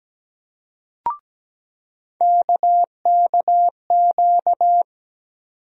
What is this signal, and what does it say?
Morse code, heard as a keyed tone, about 17 wpm, 700 Hz. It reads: KKQ
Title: USA Call Sign Postfixes - 3 Letters 17wpm